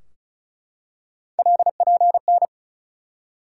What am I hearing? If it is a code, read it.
Morse code, heard as a keyed tone, about 36 wpm, 700 Hz. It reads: LPN
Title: K1USN SST QSO Practice 35wpm